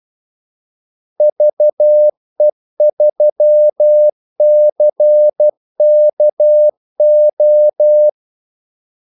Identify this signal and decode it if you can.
Morse code, heard as a keyed tone, about 12 wpm, 600 Hz. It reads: VE3CKO
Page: 2 ekstra ordmellemrum mellem kaldesignalerne